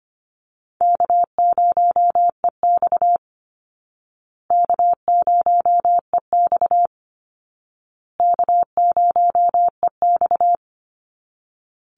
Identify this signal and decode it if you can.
Morse code, heard as a keyed tone, about 25 wpm, 700 Hz. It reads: K0EX K0EX K0EX
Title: Call Signs Repeated 2x - 25wpm